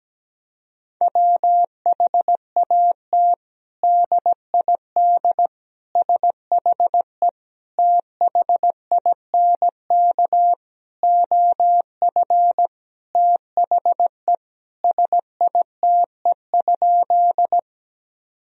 Morse code, heard as a keyed tone, about 17 wpm, 700 Hz. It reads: WHAT DID SHE THINK OF THE SITE?
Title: Sentences from Top 1000 Words - Review Part 4 - 17wpm